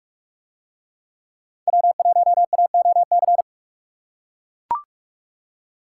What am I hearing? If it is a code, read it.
Morse code, heard as a keyed tone, about 43 wpm, 700 Hz. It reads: W1AOC